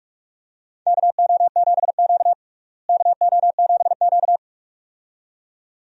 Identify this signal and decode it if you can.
Morse code, heard as a keyed tone, about 46 wpm, 700 Hz. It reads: KO7Q KO7Q
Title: USA Call Signs - 4 Characters - Repeated 2x - 45wpm